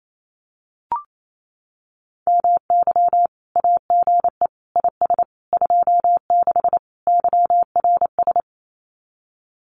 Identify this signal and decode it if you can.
Morse code, heard as a keyed tone, about 28 wpm, 700 Hz. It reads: MY AGE IS 26 YRS
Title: Common QSO Phrases Encore 28wpm